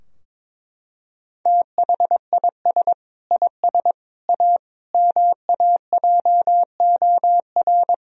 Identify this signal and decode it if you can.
Morse code, heard as a keyed tone, about 22 wpm, 700 Hz. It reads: THIS IS A MAJOR
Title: Sentences from Top 1000 Words - Review Part 4 - 22wpm